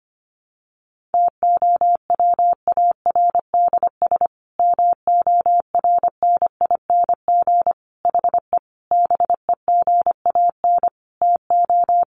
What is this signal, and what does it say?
Morse code, heard as a keyed tone, about 25 wpm, 700 Hz. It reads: TOWARDS MORNING HE BEGAN TO
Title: Sentences from Top 1100 Words 25wpm